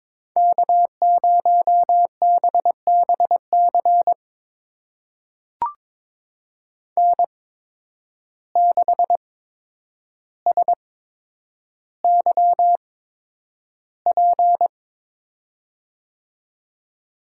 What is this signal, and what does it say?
Morse code, heard as a keyed tone, about 22 wpm, spaced as for 7 wpm, 700 Hz. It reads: K0BB CN6SYP
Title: USA Call Signs - 5 Characters - 8x Character Spacing - 22wpm